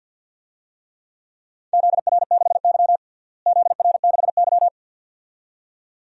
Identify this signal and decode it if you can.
Morse code, heard as a keyed tone, about 52 wpm, 700 Hz. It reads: ZR6Y ZR6Y
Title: DXCC-100 Call Signs - 4 Characters - Repeated 2x - 50wpm